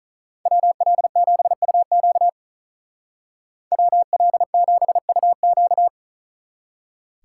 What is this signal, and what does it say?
Morse code, heard as a keyed tone, about 44 wpm, 700 Hz. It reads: WL7UQ WL7UQ